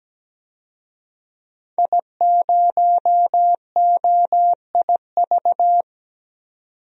Morse code, heard as a keyed tone, about 17 wpm, 700 Hz. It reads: I0OIV
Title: DXCC-100 Call Signs - 5 Characters - 17wpm